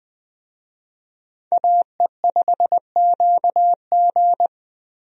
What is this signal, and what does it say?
Morse code, heard as a keyed tone, about 20 wpm, 700 Hz. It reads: AE5QG
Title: USA Call Signs - 5 Characters - Repeated 3x - 20wpm